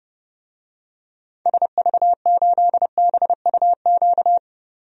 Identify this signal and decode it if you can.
Morse code, heard as a keyed tone, about 32 wpm, 700 Hz. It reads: SV8BUQ
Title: DXCC-100 Call Signs - 6 Characters - 30wpm